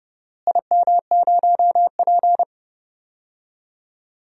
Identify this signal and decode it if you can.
Morse code, heard as a keyed tone, about 30 wpm, 700 Hz. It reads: IM0P
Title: DXCC-100 Call Signs - 4 Characters - Repeated 3x - 30wpm